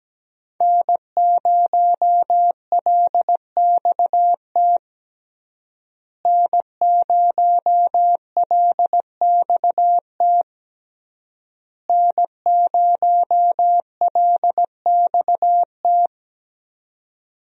Morse code, heard as a keyed tone, about 17 wpm, 700 Hz. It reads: N0LXT N0LXT N0LXT
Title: USA Call Signs - 5 Characters - Repeated 3x - 17wpm